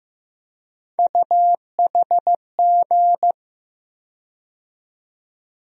Morse code, heard as a keyed tone, about 15 wpm, 700 Hz. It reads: UHG
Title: USA Call Sign Postfixes - 3 Letters 15wpm